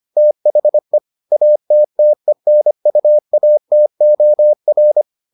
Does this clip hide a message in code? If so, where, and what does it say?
Yes: Morse code from 0.2 to 5.0 s, heard as a keyed tone, about 25 wpm, 600 Hz. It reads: THE ATTENUATOR